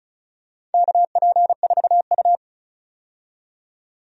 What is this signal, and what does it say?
Morse code, heard as a keyed tone, about 36 wpm, 700 Hz. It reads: KP4U
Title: DXCC-100 Call Signs - 4 Characters - 35wpm